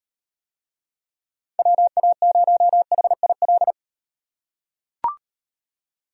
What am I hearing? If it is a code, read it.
Morse code, heard as a keyed tone, about 38 wpm, 700 Hz. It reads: WA0HIL